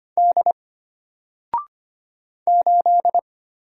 Morse code, heard as a keyed tone, about 25 wpm, 700 Hz. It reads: D 8
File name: Single-Letter-Number-25wpm